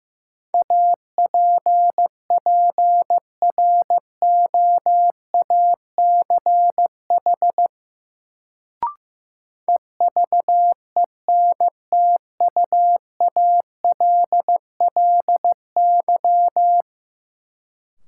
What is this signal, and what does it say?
Morse code, heard as a keyed tone, about 15 wpm, 700 Hz. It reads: APPROACH EVENTUALLY
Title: Sentences from Top 1000 Words - Review Part 6 - New Word Drill - 15wpm